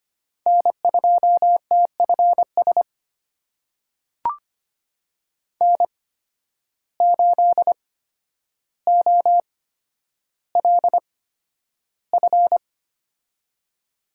Morse code, heard as a keyed tone, about 25 wpm, spaced as for 11 wpm, 700 Hz. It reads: N2TFS N8OLF
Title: USA Call Signs - 8x Character Spacing - 25wpm